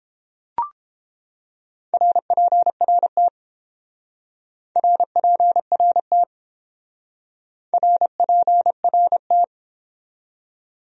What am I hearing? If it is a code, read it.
Morse code, heard as a keyed tone, about 28 wpm, 700 Hz. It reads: RPRT RPRT RPRT